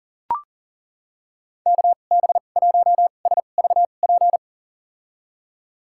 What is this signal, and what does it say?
Morse code, heard as a keyed tone, about 40 wpm, spaced as for 29 wpm, 700 Hz. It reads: KB1SVP
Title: USA Call Signs - 6 Characters - 2x Character Spacing - 40wpm